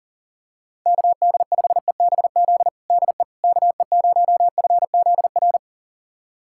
Morse code, heard as a keyed tone, about 40 wpm, 700 Hz. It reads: KD5EBZ DE KE0FZR
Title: USA Callsign DE Callsign - 6 Characters - 40wpm